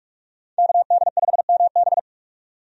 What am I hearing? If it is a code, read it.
Morse code, heard as a keyed tone, about 46 wpm, 700 Hz. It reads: KD5MB
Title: USA Call Signs - 5 Characters - 45wpm